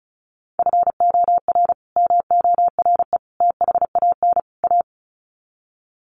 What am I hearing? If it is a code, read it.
Morse code, heard as a keyed tone, about 34 wpm, 700 Hz. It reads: FOR MORE THAN A
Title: Sets of 4 Words 35wpm